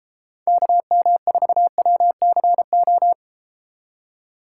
Morse code, heard as a keyed tone, about 32 wpm, 700 Hz. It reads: KM4WCO